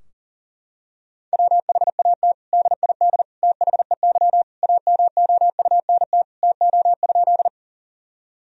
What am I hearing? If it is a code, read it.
Morse code, heard as a keyed tone, about 40 wpm, 700 Hz. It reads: WHAT DID THEY AMOUNT TO?
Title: Sentences from Top 1000 Words - Review Part 4 - 40wpm